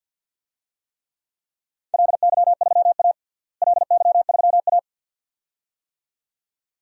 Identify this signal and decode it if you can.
Morse code, heard as a keyed tone, about 50 wpm, 700 Hz. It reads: LY3A LY3A
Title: DXCC-100 Call Signs - 4 Characters - Repeated 2x - 50wpm